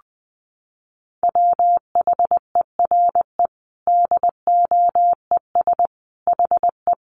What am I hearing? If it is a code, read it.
Morse code, heard as a keyed tone, about 19 wpm, 700 Hz. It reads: WHERE DOES HE
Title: Sentences from Top 800 Words 20wpm